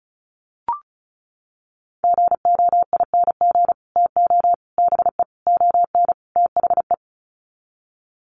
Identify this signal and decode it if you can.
Morse code, heard as a keyed tone, about 34 wpm, 700 Hz. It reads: GOING TO BE ON THE
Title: Sets of 5 Words Encore2 35wpm